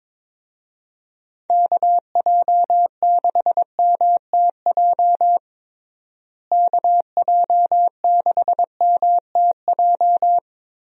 Morse code, heard as a keyed tone, about 22 wpm, 700 Hz. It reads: KJ6MTJ KJ6MTJ
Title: USA Call Signs - 6 Characters - Repeated 3x - 22wpm